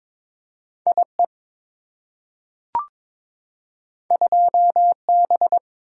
Morse code, heard as a keyed tone, about 22 wpm, 690 Hz. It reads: IE 2B
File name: practice / 22wpm / Two Letter-Number Combinations - 22wpm